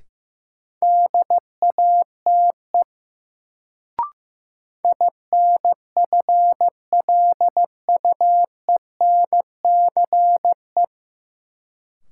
Morse code, heard as a keyed tone, about 16 wpm, 700 Hz. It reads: DATE INFLUENCE
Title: Sentences from Top 1100 Words - New Word Drill Alt - 15wpm